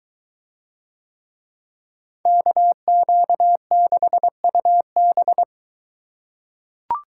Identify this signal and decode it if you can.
Morse code, heard as a keyed tone, about 23 wpm, 700 Hz. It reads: KQ6UB